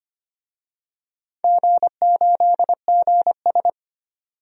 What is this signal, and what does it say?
Morse code, heard as a keyed tone, about 25 wpm, 700 Hz. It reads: G8GS
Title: DXCC-100 Call Signs - 4 Characters - 25wpm